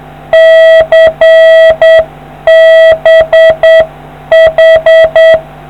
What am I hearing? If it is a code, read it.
Morse code, heard as a keyed tone, about 7 wpm, 650 Hz. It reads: CBH